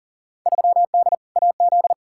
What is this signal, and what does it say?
Morse code, heard as a keyed tone, about 40 wpm, 700 Hz. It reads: 3D AZ
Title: Field Day Exchanges - 40wpm